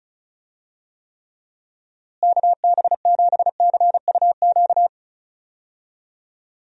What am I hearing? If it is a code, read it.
Morse code, heard as a keyed tone, about 36 wpm, 700 Hz. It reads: KB7CUQ
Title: USA Call Signs - 6 Characters - Repeated 3x - 35wpm